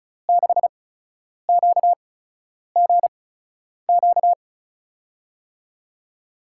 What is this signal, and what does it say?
Morse code, heard as a keyed tone, about 36 wpm, spaced as for 10 wpm, 700 Hz. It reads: 6QGQ
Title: USA Call Signs - 6 Characters - 8x Character Spacing - 35wpm